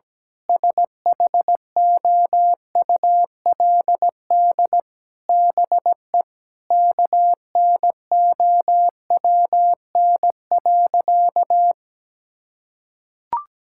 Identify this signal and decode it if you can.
Morse code, heard as a keyed tone, about 17 wpm, 700 Hz. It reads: SHOULD BE KNOWN.
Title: Sentences from Top 1000 Words - Review Part 6 - 17wpm